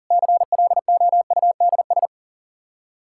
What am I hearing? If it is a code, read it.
Morse code, heard as a keyed tone, about 40 wpm, 700 Hz. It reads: CLOUDS
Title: Practice QSO - Exchange 2 - Info - 40wpm